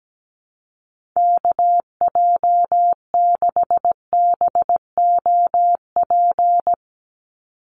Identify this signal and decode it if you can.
Morse code, heard as a keyed tone, about 17 wpm, 700 Hz. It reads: KJ6BOP
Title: USA Call Signs 17wpm